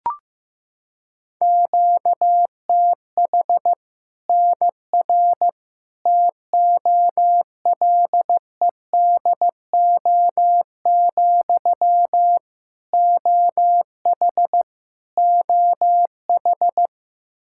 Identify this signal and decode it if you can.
Morse code, heard as a keyed tone, about 15 wpm, 700 Hz. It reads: QTH NR TOLEDO, OH OH
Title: Practice QSO - Exchange 1 - Beginning - 15wpm